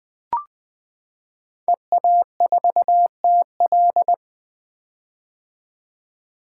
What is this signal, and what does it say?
Morse code, heard as a keyed tone, about 20 wpm, 700 Hz. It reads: EA4TL
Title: 2023 POTA Hunter Call Signs - 20wpm